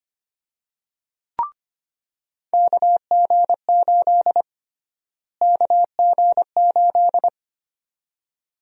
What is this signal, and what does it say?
Morse code, heard as a keyed tone, about 25 wpm, 700 Hz. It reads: KG8 KG8
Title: USA Call Sign Prefixes - Repeated 2x - 25wpm